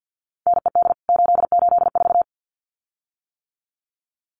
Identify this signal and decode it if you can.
Morse code, heard as a keyed tone, about 50 wpm, 700 Hz. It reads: NED 884
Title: CWT Exchange 50wpm